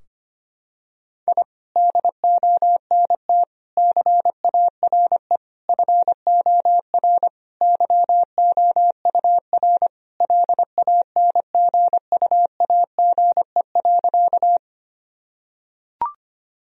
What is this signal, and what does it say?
Morse code, heard as a keyed tone, about 25 wpm, 700 Hz. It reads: I DONT CARE FOR YOUR LANGUAGE.
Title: Sentences from Top 1000 Words - Review Part 3 - 25wpm